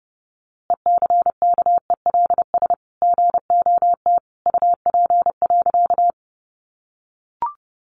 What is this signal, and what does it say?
Morse code, heard as a keyed tone, about 30 wpm, 700 Hz. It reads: ECKELS GOT UP.
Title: A Sound Of Thunder 30wpm